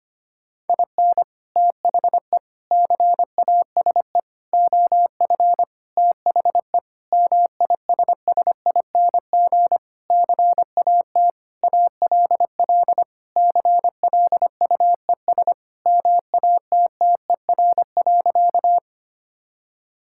Morse code, heard as a keyed tone, about 25 wpm, 700 Hz. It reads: IN THE CASE OF THE MISSING CAT ALL CLUES MATTER.
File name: practice / 25wpm / Sentences with 4-Word Phrases - 25wpm